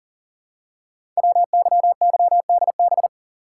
Morse code, heard as a keyed tone, about 39 wpm, 700 Hz. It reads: WYYDB
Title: Code Groups - 5 Letters - 40wpm